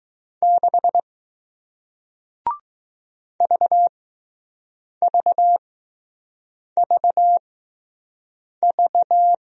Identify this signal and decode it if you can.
Morse code, heard as a keyed tone, about 20 wpm, spaced as for 6 wpm, 700 Hz. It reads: 6 VVVV